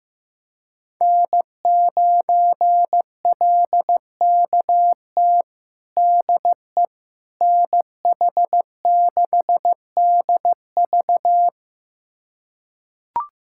Morse code, heard as a keyed tone, about 15 wpm, 700 Hz. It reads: N9LKT DE NH6DV
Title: USA Callsign DE Callsign - 5 Characters - 15wpm